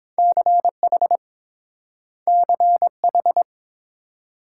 Morse code, heard as a keyed tone, about 30 wpm, 690 Hz. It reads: CH CH